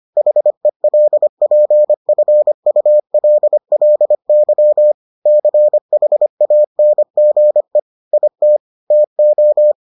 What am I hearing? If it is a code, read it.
Morse code, heard as a keyed tone, about 25 wpm, 600 Hz. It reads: HELPFULLY CHANGE IT TO